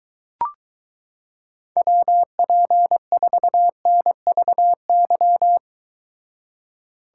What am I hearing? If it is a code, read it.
Morse code, heard as a keyed tone, about 22 wpm, 700 Hz. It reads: WP4NVY